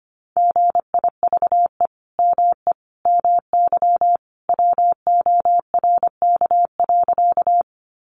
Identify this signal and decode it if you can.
Morse code, heard as a keyed tone, about 25 wpm, 700 Hz. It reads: GIVE ME MY WORK.
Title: Sentences from Top 200 Words 25wpm